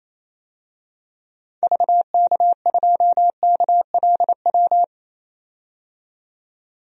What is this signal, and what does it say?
Morse code, heard as a keyed tone, about 28 wpm, 700 Hz. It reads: VK2KLW
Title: DXCC-100 Call Signs - 6 Characters - 28wpm